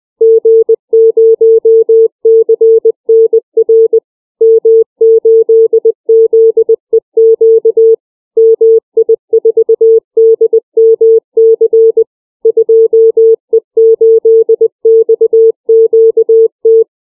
Morse code, heard as a keyed tone, about 20 wpm, 450 Hz. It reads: G0CNR M8ZEQ MI4DMC 2E8XQT
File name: CW / callsign